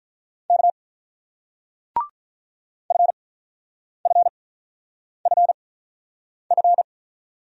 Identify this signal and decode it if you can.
Morse code, heard as a keyed tone, about 44 wpm, spaced as for 9 wpm, 700 Hz. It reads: K FFFF